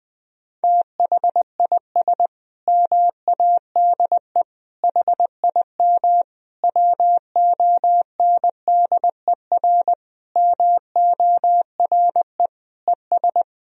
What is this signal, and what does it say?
Morse code, heard as a keyed tone, about 20 wpm, 700 Hz. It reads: THIS MADE HIM WONDER MORE ES
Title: Sentences from Top 500 Words and Abbreviations 20wpm